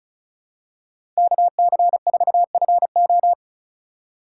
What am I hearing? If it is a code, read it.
Morse code, heard as a keyed tone, about 34 wpm, 700 Hz. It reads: KC4FO
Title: USA Call Signs - 5 Characters - 2x Character Spacing - 35wpm